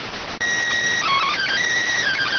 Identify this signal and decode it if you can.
SSTV (slow-scan television), PD 290.